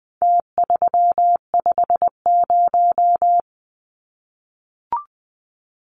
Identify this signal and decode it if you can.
Morse code, heard as a keyed tone, about 20 wpm, 700 Hz. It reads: T350
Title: Numbers - Contest Style - 4-Digit 20wpm